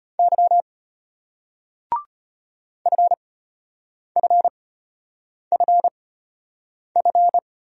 Morse code, heard as a keyed tone, about 33 wpm, spaced as for 8 wpm, 700 Hz. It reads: Y FFFF